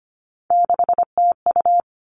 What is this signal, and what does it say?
Morse code, heard as a keyed tone, about 25 wpm, 700 Hz. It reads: 6TU
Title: All Cut Numbers - 3-Digit 25wpm